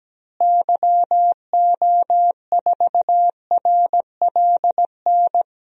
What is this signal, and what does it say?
Morse code, heard as a keyed tone, about 17 wpm, 700 Hz. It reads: YO4RLN
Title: DXCC-100 Call Signs - 6 Characters - Repeated 2x - 17wpm